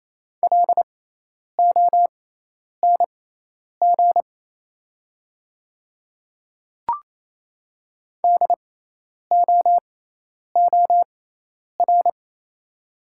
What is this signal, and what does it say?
Morse code, heard as a keyed tone, about 27 wpm, spaced as for 10 wpm, 690 Hz. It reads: LONG DOOR
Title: 6x Character Spacing - Top 300 Words - 28wpm